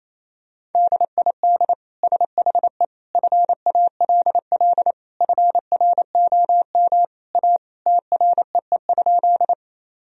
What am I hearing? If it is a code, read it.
Morse code, heard as a keyed tone, about 28 wpm, 700 Hz. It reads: DID SHE FALL FROM A TREE?
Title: Sentences from Top 1000 Words - Review Part 2 - 28wpm